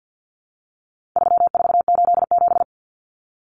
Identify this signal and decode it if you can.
Morse code, heard as a keyed tone, about 50 wpm, 700 Hz. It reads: 3487
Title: Numbers - 4-Digit 50wpm